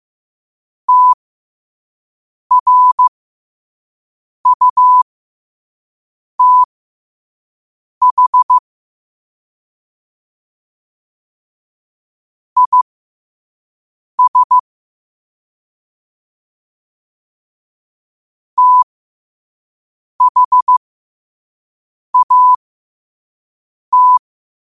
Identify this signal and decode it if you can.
Morse code, heard as a keyed tone, about 15 wpm, spaced as for 5 wpm, 1000 Hz. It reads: TRUTH IS THAT
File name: qotdfiles / 5wpm